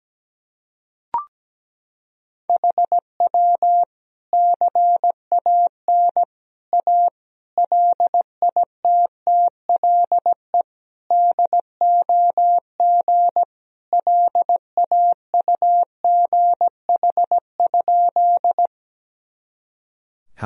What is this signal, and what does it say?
Morse code, heard as a keyed tone, about 17 wpm, 700 Hz. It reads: HW CAN A LITTLE DOG LAUGH?
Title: Sentences from Top 500 Words and Abbreviations 17wpm